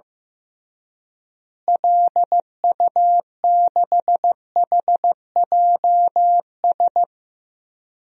Morse code, heard as a keyed tone, about 15 wpm, 700 Hz. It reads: LU6HJS
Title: DXCC-100 Call Signs - 6 Characters - Repeated 3x - 15wpm